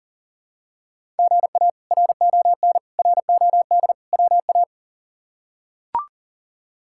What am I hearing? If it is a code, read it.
Morse code, heard as a keyed tone, about 37 wpm, 700 Hz. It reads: GA RON ROD WA
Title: K1USN SST QSO Practice - 40wpm